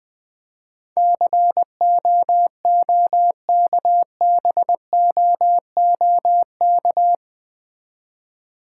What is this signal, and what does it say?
Morse code, heard as a keyed tone, about 20 wpm, 700 Hz. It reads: COOKBOOK
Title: Difficult Words with Hints - 20wpm